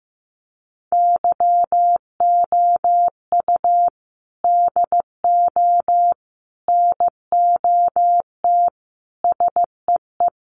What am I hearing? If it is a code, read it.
Morse code, heard as a keyed tone, about 15 wpm, 700 Hz. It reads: YOU DO NOT SEE
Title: Sentences from Top 1300 Words 15wpm